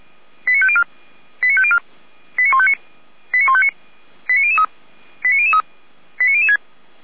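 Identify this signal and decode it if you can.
Five-tone selective call (ZVEI), heard as quick runs of five tones, tones of about 70 ms. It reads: 89504 89504 89159 89159 89003 89003 89006
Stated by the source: ZVEI five-tone selective call received via SDR, address 89150.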